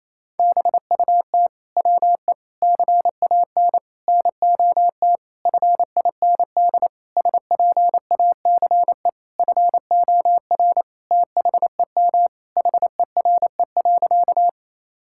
Morse code, heard as a keyed tone, about 28 wpm, 700 Hz. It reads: BUT WE CAN NOT FIND SPACE FOR THEM HERE.
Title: Sentences from Top 1000 Words - Review Part 5 - 28wpm